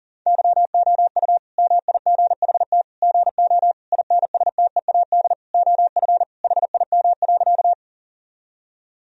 Morse code, heard as a keyed tone, about 40 wpm, 700 Hz. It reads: YOU MIGHT GO INSTEAD OF HIM.
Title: Sentences from Top 1000 Words - Review Part 4 - 40wpm